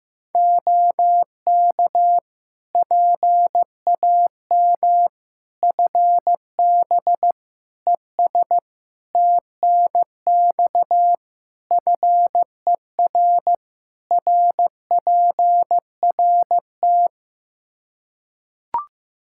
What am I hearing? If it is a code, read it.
Morse code, heard as a keyed tone, about 15 wpm, 700 Hz. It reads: OK PAM FB ES TNX FER RPRT
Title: K1USN SST QSO Practice 15wpm